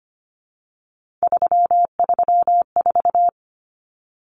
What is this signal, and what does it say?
Morse code, heard as a keyed tone, about 25 wpm, 700 Hz. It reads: 334
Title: Common Cut Numbers - 3-Digit 25wpm